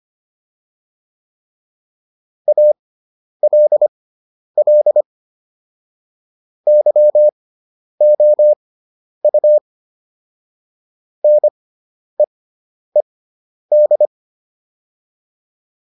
Morse code, heard as a keyed tone, about 25 wpm, spaced as for 10 wpm, 600 Hz. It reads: ALL YOU NEED